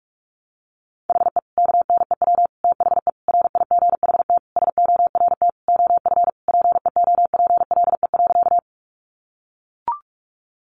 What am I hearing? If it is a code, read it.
Morse code, heard as a keyed tone, about 44 wpm, 700 Hz. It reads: HE KNEW THE RIGHT SORT OF PEOPLE.
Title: Sentences from Top 1300 Words 45wpm